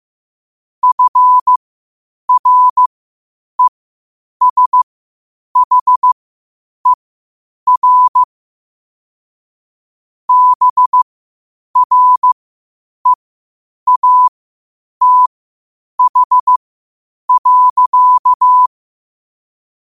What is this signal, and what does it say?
Morse code, heard as a keyed tone, about 15 wpm, spaced as for 8 wpm, 1000 Hz. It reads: FRESHER BREATH.
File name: qotdfiles / 8wpm